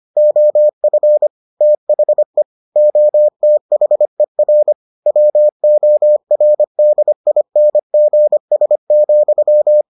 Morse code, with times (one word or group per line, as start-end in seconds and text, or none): 0.2-1.3 OF
1.6-2.4 THE
2.8-4.7 OTHER
5.1-9.8 WORDINGS,